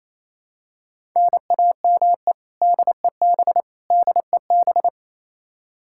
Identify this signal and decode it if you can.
Morse code, heard as a keyed tone, about 28 wpm, 700 Hz. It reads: NAME DEB DEB
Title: Practice QSO - Exchange 1 - Beginning - 28wpm